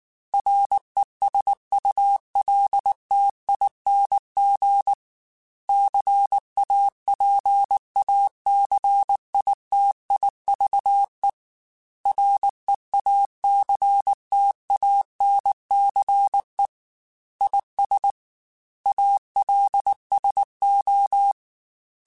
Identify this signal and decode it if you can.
Morse code, heard as a keyed tone, about 19 wpm, 800 Hz. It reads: RESULTING CAPACITIVE REACTANCE IS ALSO
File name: Technical Subjects / Technical Subjects-mp3s / XC @18